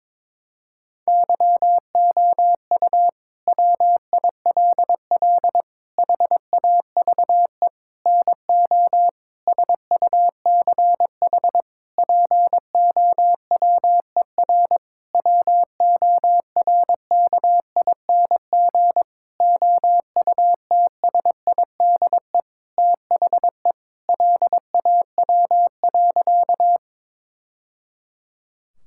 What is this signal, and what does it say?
Morse code, heard as a keyed tone, about 22 wpm, 700 Hz. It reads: YOU WILL HAVE NO SUCH POWER WORKING OUTSIDE THE LAW.
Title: Sentences from Top 1000 Words - Review Part 6 - 22wpm